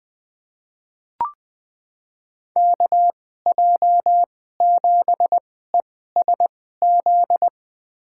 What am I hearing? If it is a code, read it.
Morse code, heard as a keyed tone, about 20 wpm, spaced as for 14 wpm, 700 Hz. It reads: KJ7ESZ
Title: USA Call Signs - 2x Character Spacing - 20wpm